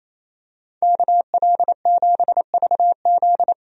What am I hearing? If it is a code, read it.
Morse code, heard as a keyed tone, about 27 wpm, 700 Hz. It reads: KL7VZ
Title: DXCC-100 Call Signs - 5 Characters - 28wpm